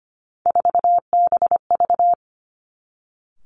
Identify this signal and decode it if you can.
Morse code, heard as a keyed tone, about 25 wpm, 700 Hz. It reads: 4BV
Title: Three Challenging Letter-Number Combinations 25wpm